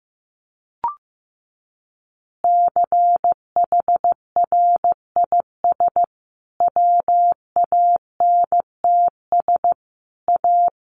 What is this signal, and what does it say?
Morse code, heard as a keyed tone, about 15 wpm, 700 Hz. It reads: CHRIS WANTS A
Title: Easy Phrases 15wpm